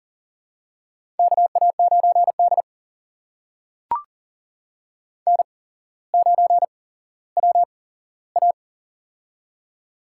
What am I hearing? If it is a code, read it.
Morse code, heard as a keyed tone, about 38 wpm, spaced as for 17 wpm, 700 Hz. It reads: KA9D N9WA
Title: USA Call Signs - 4 Characters - 8x Character Spacing - 40wpm